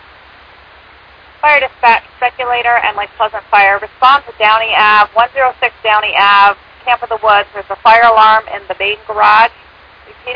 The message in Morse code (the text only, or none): none